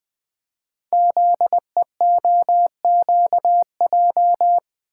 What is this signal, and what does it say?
Morse code, heard as a keyed tone, about 20 wpm, 700 Hz. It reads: ZEOQJ
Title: Code Groups - 5 Letters - 20wpm